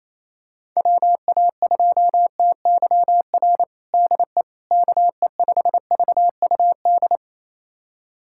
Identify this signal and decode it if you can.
Morse code, heard as a keyed tone, about 27 wpm, 700 Hz. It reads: WA2TYR DE KE5VUD
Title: USA Callsign DE Callsign - 6 Characters - 28wpm